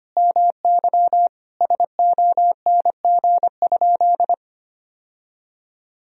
Morse code, heard as a keyed tone, about 25 wpm, 700 Hz. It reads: MY SONG?
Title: Sentences from Top 1000 Words - Review Part 2 - 25wpm